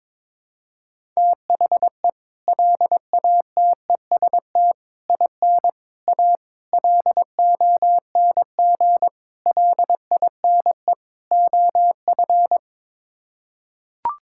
Morse code, heard as a keyed tone, about 22 wpm, 700 Hz. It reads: THE LATEST IN A LONG LINE OF
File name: practice / 22wpm / Sets of 7 Words - 22wpm